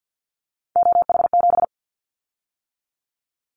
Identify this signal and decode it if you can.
Morse code, heard as a keyed tone, about 50 wpm, 700 Hz. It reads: OH7
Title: Callsign Prefixes for Top 50 DXCC - 50wpm